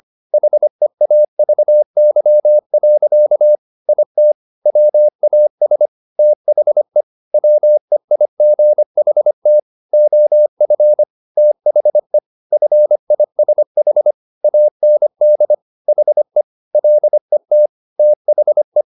Morse code, heard as a keyed tone, about 25 wpm, 600 Hz. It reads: HEAVY. IT WAS THE WEIGHT OF THE FISH AND HE LET THE
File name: the old man and the sea part 1 25wpm